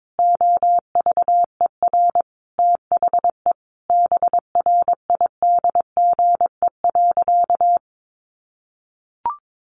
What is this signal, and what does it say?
Morse code, heard as a keyed tone, about 22 wpm, 700 Hz. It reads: OVER THE BRIDGE.
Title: Sentences from Top 1500 Words 22wpm